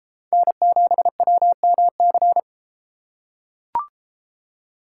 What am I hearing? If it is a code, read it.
Morse code, heard as a keyed tone, about 32 wpm, 700 Hz. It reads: N7WMC